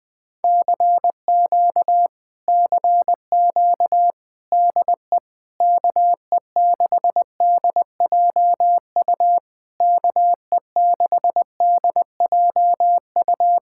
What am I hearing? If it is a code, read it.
Morse code, heard as a keyed tone, about 20 wpm, 700 Hz. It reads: CQ CQ DE KE6DJU KE6DJU
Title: Speed-Racing Common QSO Phrases Encore - 17wpm